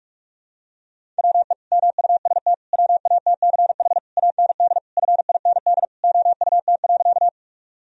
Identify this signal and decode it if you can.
Morse code, heard as a keyed tone, about 46 wpm, 700 Hz. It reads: WE MUST WATCH AND FIND OUT.
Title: Sentences from Top 1000 Words - Review Part 1 - 45wpm